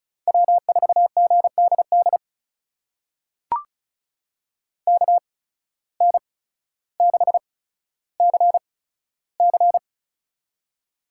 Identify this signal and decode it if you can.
Morse code, heard as a keyed tone, about 34 wpm, spaced as for 15 wpm, 690 Hz. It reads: W4GDD KN6CC
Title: USA Call Signs - 5 Characters - 8x Character Spacing - 35wpm